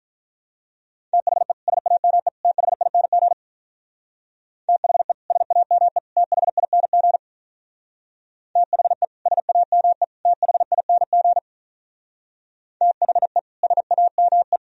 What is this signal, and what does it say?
Morse code, heard as a keyed tone, about 44 wpm, 700 Hz. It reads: THE SAME THING THE SAME THING THE SAME THING THE SAME
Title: Speed-Racing - Sets of 3 Words 35wpm